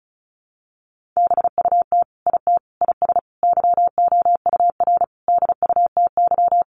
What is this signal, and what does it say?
Morse code, heard as a keyed tone, about 34 wpm, 700 Hz. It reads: BUT IT IS YOUR DUTY
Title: Sentences from Top 1500 Words 35wpm